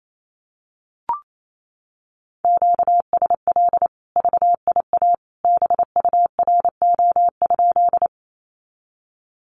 Morse code, heard as a keyed tone, about 28 wpm, 700 Hz. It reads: QSL VIA BURO?
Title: Common QSO Phrases 28wpm